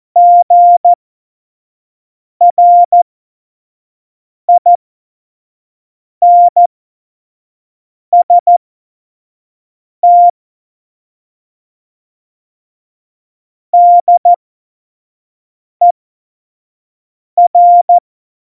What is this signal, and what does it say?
Morse code, heard as a keyed tone, about 14 wpm, spaced as for 5 wpm, 700 Hz. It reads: GRINST DER